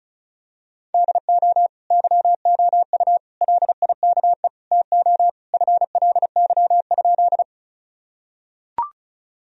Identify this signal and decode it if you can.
Morse code, heard as a keyed tone, about 36 wpm, 700 Hz. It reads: DO YOU LIKE TO FLY?
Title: Sentences from Top 1000 Words - Review Part 2 - 35wpm